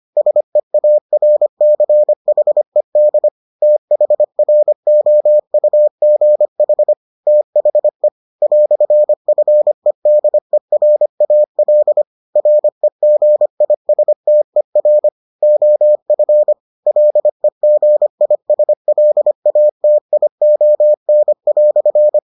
Morse code, heard as a keyed tone, about 25 wpm, 600 Hz. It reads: SEARCHED THROUGH THE "FEDERAL REGISTER OF LEGISLATION"